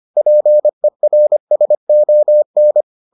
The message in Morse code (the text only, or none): PERSON